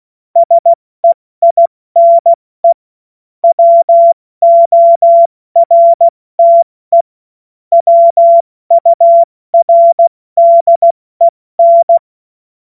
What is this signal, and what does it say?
Morse code, heard as a keyed tone, about 16 wpm, 680 Hz. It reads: SEINE WORTE WURDEN
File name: Morsekurs / Witze / Witz 30. August 2022 70 F80